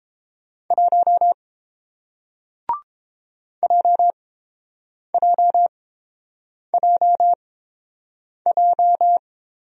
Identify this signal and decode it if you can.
Morse code, heard as a keyed tone, about 30 wpm, spaced as for 7 wpm, 700 Hz. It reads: 1 JJJJ